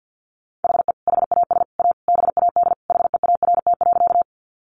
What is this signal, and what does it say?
Morse code, heard as a keyed tone, about 50 wpm, 700 Hz. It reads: HE HAS A BAD HEART.
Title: Sentences from Top 1200 Words 50wpm